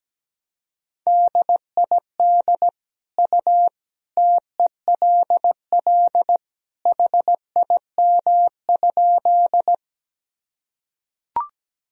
Morse code, heard as a keyed tone, about 17 wpm, 700 Hz. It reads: DID U TELL HIM?
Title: Sentences from Top 200 Words and Abbreviations 17wpm